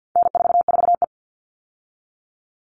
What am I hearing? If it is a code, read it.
Morse code, heard as a keyed tone, about 50 wpm, 700 Hz. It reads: N44E
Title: All Cut Numbers - 4-Digit 50wpm